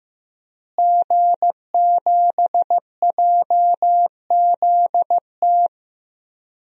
Morse code, heard as a keyed tone, about 15 wpm, 700 Hz. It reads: G7JZT
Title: DXCC-100 Call Signs - 5 Characters - Repeated 2x - 15wpm